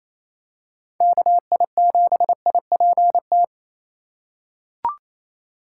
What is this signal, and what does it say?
Morse code, heard as a keyed tone, about 28 wpm, 690 Hz. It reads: KI7IPT
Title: USA Call Signs - 3x Character Spacing - 28wpm